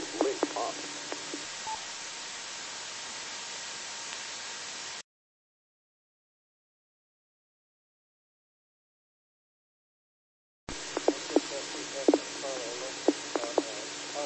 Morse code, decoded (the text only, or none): none